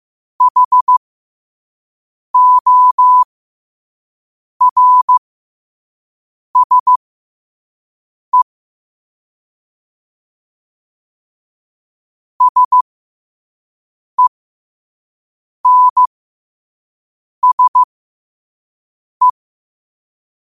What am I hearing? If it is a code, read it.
Morse code, heard as a keyed tone, about 15 wpm, spaced as for 5 wpm, 1000 Hz. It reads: HORSE SENSE
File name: qotdfiles / 5wpm